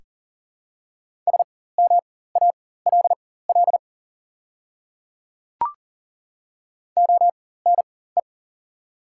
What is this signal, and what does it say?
Morse code, heard as a keyed tone, about 41 wpm, spaced as for 19 wpm, 700 Hz. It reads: SMALL ONE
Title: 4x Character Spacing - Top 200 Words - 40wpm